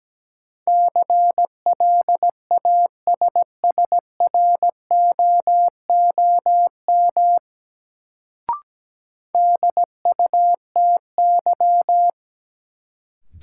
Morse code, heard as a keyed tone, about 17 wpm, 700 Hz. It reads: CLASSROOM DUTY
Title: Sentences from Top 1500 Words - New Word Drill Alt - 17wpm